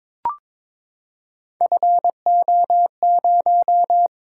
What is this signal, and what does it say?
Morse code, heard as a keyed tone, about 23 wpm, 700 Hz. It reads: FO0
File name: practice / 22wpm / Three Letter-Number Combinations - 22wpm